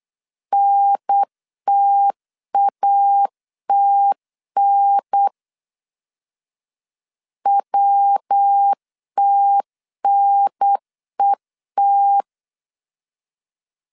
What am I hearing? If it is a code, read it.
Morse code, heard as a keyed tone, about 8 wpm, 800 Hz. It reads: NTATN WTNET